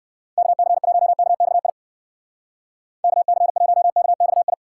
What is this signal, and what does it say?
Morse code, heard as a keyed tone, about 58 wpm, 700 Hz. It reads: KC1KXI KC1KXI